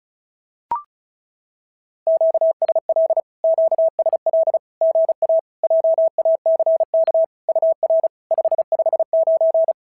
Morse code, heard as a keyed tone, about 36 wpm, 650 Hz. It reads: QSL QSL GA JACK UR 559
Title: SOTA QSO Practice 35wpm